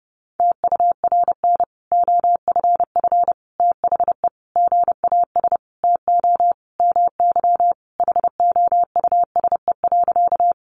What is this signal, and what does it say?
Morse code, heard as a keyed tone, about 30 wpm, 700 Hz. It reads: TURN OFF THE GAS TO MY HOUSE.
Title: Sentences from Top 1100 Words 30wpm